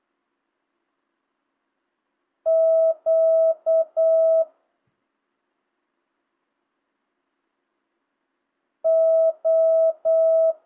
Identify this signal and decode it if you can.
Morse code, heard as a keyed tone, about 8 wpm, 650 Hz. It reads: Q O